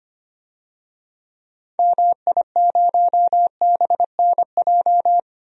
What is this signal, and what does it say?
Morse code, heard as a keyed tone, about 25 wpm, 690 Hz. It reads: MI0BNJ
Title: DXCC-100 Call Signs - 6 Characters - 25wpm